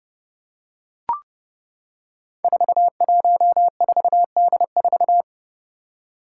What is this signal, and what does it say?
Morse code, heard as a keyed tone, about 30 wpm, 700 Hz. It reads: 414D4
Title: Five Challenging Letter-Number Combinations 30wpm